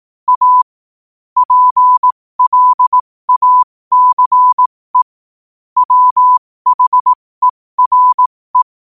Morse code, heard as a keyed tone, about 18 wpm, spaced as for 15 wpm, 1000 Hz. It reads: A PLACE WHERE
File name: 15wpm